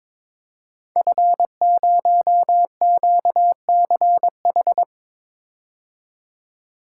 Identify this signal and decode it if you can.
Morse code, heard as a keyed tone, about 22 wpm, 690 Hz. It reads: F0QCH
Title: Code Groups - 5 Letter:Numbers - 22wpm